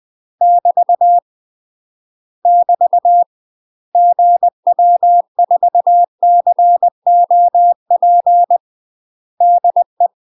Morse code, heard as a keyed tone, about 20 wpm, 700 Hz. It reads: <BT> <BT> GW4COP DE
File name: UK MASTER QSOs TRAINING / 20WPM / UK MASTER QSOs 0049